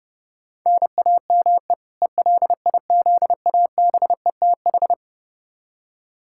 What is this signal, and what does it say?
Morse code, heard as a keyed tone, about 30 wpm, 700 Hz. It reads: NAME ELIZABETH
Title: K1USN SST QSO Practice 30wpm